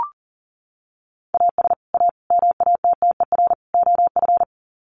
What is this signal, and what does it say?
Morse code, heard as a keyed tone, about 38 wpm, 700 Hz. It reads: AS A MATTER OF
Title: Sets of 4 Words 40wpm